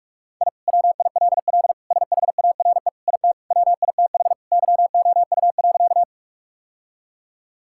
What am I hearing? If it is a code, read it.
Morse code, heard as a keyed tone, about 46 wpm, 700 Hz. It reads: I WILL SHARE IT WITH YOU.
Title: Sentences from Top 1000 Words - Review Part 6 - 45wpm